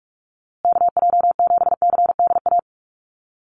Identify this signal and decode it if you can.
Morse code, heard as a keyed tone, about 44 wpm, 700 Hz. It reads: KJ7CDA
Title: USA Call Signs Repeated 2x 45wpm